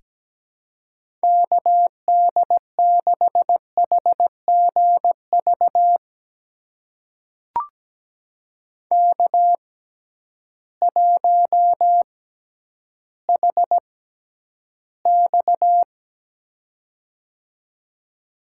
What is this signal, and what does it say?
Morse code, heard as a keyed tone, about 17 wpm, spaced as for 10 wpm, 700 Hz. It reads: KD6HGV K1HX
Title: USA Call Signs - 6x Character Spacing - 17wpm